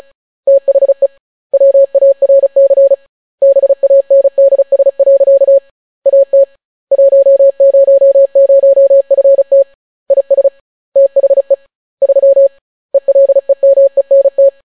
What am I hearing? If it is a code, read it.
Morse code, heard as a keyed tone, about 36 wpm, 550 Hz. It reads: THE WARC BANDS. AT 100FT IS THE 3 ELEMENT